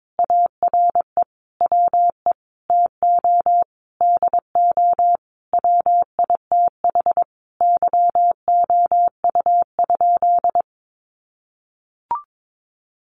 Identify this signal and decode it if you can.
Morse code, heard as a keyed tone, about 22 wpm, 700 Hz. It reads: ARE WE TO DO WITH YOU?
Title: Sentences from Top 100 Words 22wpm